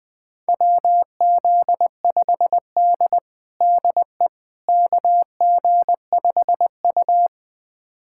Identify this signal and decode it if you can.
Morse code, heard as a keyed tone, about 20 wpm, 700 Hz. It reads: WZ5D DE KG5U
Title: USA Callsign DE Callsign - 4 Characters - 20wpm